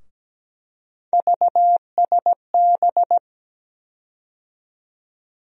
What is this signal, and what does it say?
Morse code, heard as a keyed tone, about 17 wpm, 700 Hz. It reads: VSB
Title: USA Call Sign Postfixes - 3 Letters 17wpm